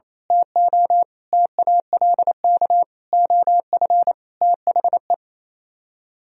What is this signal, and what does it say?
Morse code, heard as a keyed tone, about 27 wpm, 700 Hz. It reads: TO TALK OF THE
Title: Speed-Racing - Sets of 5 Words 28wpm